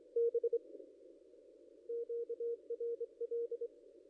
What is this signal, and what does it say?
Morse code, heard as a keyed tone, about 25 wpm, 470 Hz. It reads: B QRL
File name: contests / ss-cw-k5tr-op-n5rz -0800